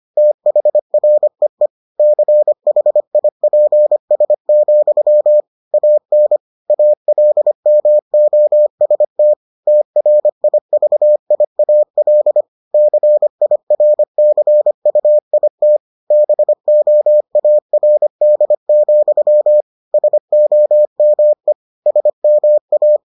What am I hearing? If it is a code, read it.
Morse code, heard as a keyed tone, about 25 wpm, 600 Hz. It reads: THREE CHIPS, AN ALMOST TRIVIAL CIRCUIT BOARD, SOME SMA